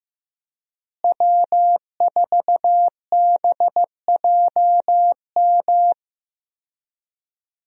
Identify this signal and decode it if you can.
Morse code, heard as a keyed tone, about 15 wpm, 700 Hz. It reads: W4BJM
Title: USA Call Signs - 5 Characters - Repeated 3x - 15wpm